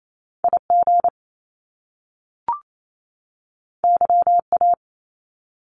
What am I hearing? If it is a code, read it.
Morse code, heard as a keyed tone, about 28 wpm, 700 Hz. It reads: IG YA
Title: Top Two Letter Combinations in Words 28wpm